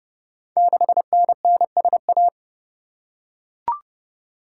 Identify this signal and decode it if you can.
Morse code, heard as a keyed tone, about 30 wpm, 700 Hz. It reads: 6NNSA
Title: Code Groups - 5 Letter:Numbers - 30wpm